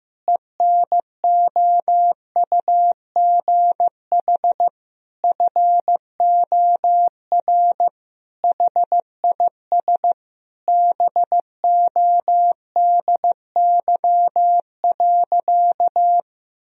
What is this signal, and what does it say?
Morse code, heard as a keyed tone, about 15 wpm, 700 Hz. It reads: ENOUGH FOR HIS BODY.
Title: Sentences from Top 1000 Words - Review Part 2 - 15wpm